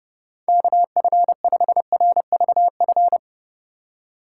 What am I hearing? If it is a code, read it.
Morse code, heard as a keyed tone, about 30 wpm, 700 Hz. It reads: KF5RVF
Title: USA Call Signs - 3x Character Spacing - 30wpm